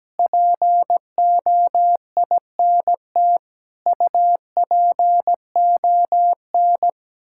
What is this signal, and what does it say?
Morse code, heard as a keyed tone, about 17 wpm, 700 Hz. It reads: POINT UPON
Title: Sentences from Top 1600 Words 17wpm